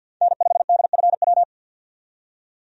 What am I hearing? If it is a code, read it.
Morse code, heard as a keyed tone, about 50 wpm, 700 Hz. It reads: N5DFW
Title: USA Call Signs - 5 Characters - 50wpm